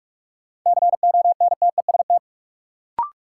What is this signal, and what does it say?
Morse code, heard as a keyed tone, about 41 wpm, 700 Hz. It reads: CONTEST
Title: Sentences from Top 1600 Words - New Word Drill - 45wpm